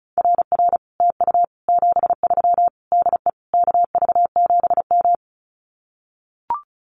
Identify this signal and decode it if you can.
Morse code, heard as a keyed tone, about 34 wpm, 700 Hz. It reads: RR TU 73 DE KV7M
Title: Common QSO Phrases Encore 35wpm